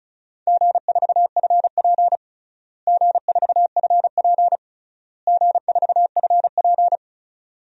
Morse code, heard as a keyed tone, about 34 wpm, 700 Hz. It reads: G4FP G4FP G4FP
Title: DXCC-100 Call Signs - 4 Characters - Repeated 3x - 35wpm